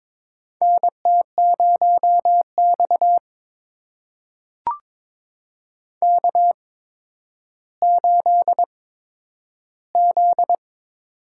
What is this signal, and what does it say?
Morse code, heard as a keyed tone, about 22 wpm, spaced as for 11 wpm, 700 Hz. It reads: NT0X K8Z
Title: USA Call Signs - 4 Characters - 8x Character Spacing - 22wpm